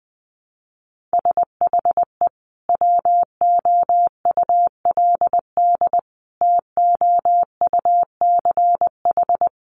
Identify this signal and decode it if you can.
Morse code, heard as a keyed tone, about 20 wpm, 700 Hz. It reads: SHE WOULD TOUCH
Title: Sentences from Top 1200 Words 20wpm